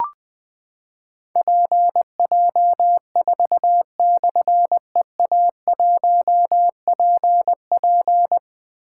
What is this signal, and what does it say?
Morse code, heard as a keyed tone, about 20 wpm, 700 Hz. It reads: PJ4/EA1PP
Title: Difficult Call Signs 20wpm